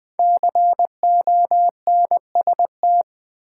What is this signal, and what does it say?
Morse code, heard as a keyed tone, about 20 wpm, 700 Hz. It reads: CONST
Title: Build A Word - 5 Letters At A Time - 12-Letter Words - 20wpm